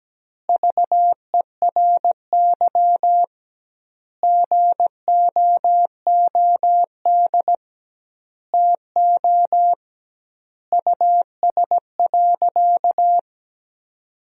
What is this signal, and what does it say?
Morse code, heard as a keyed tone, about 17 wpm, 700 Hz. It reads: VERY GOOD TO US.
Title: 2x Word-Spacing - Sentences from Top 200 Words - 17wpm